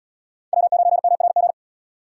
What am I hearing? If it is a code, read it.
Morse code, heard as a keyed tone, about 73 wpm, 700 Hz. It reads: W9MRG